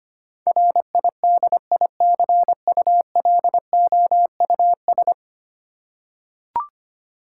Morse code, heard as a keyed tone, about 25 wpm, 700 Hz. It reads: RIDICULOUS
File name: Ten Letter Words - 25wpm